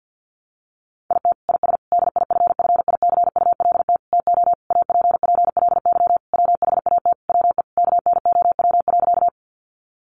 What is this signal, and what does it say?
Morse code, heard as a keyed tone, about 50 wpm, 700 Hz. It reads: IT IS DIFFICULT TO APPLY WHAT WE KNOW.
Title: Sentences from Top 1200 Words 50wpm